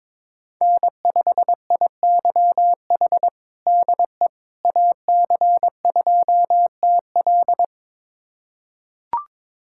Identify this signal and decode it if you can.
Morse code, heard as a keyed tone, about 22 wpm, 700 Hz. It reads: N5IYH DE AC2TL
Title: USA Callsign DE Callsign - 5 Characters - 22wpm